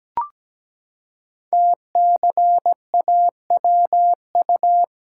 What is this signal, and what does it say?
Morse code, heard as a keyed tone, about 17 wpm, 700 Hz. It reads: TCAWU
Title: Code Groups - 5 Letter:Numbers - 17wpm